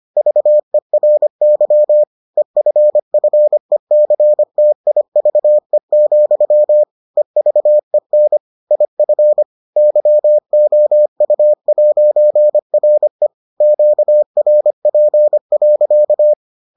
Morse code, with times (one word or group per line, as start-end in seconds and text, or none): 0.2-2.0 VERY
2.4-6.8 EFFECTIVE,
7.2-8.4 EVEN
8.7-9.4 IF
9.8-13.3 YOU'RE
13.6-16.3 QRP.